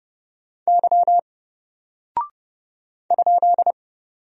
Morse code, heard as a keyed tone, about 30 wpm, 700 Hz. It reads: Y ?
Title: Single Character (Letter:Number:Punctuation:Prosign) Alt 30wpm